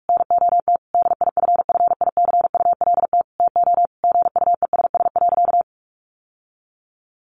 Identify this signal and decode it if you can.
Morse code, heard as a keyed tone, about 44 wpm, 700 Hz. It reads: NOT DIFFICULT TO GUESS.
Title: Sentences from Top 1500 Words 45wpm